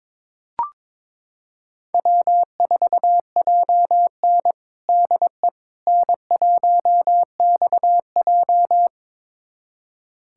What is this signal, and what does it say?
Morse code, heard as a keyed tone, about 22 wpm, 700 Hz. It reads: W4JN DE N1XJ
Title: USA Callsign DE Callsign - 4 Characters - 22wpm